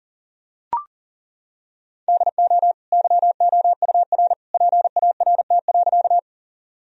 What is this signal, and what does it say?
Morse code, heard as a keyed tone, about 37 wpm, 700 Hz. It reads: DO YOUR PART.
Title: Sentences from Top 1000 Words - Review Part 1 - 40wpm